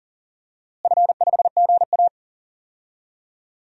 Morse code, heard as a keyed tone, about 39 wpm, 700 Hz. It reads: F5GA
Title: DXCC-100 Call Signs - 4 Characters - Repeated 3x - 40wpm